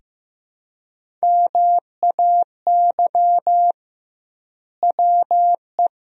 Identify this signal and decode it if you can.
Morse code, heard as a keyed tone, about 15 wpm, 700 Hz. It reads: MAY WE
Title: 2x Word-Spacing - Sentences from Top 100 Words - 15wpm